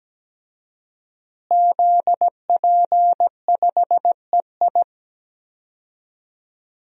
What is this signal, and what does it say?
Morse code, heard as a keyed tone, about 17 wpm, 700 Hz. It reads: ZP5EI
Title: DXCC-100 Call Signs - 5 Characters - Repeated 2x - 17wpm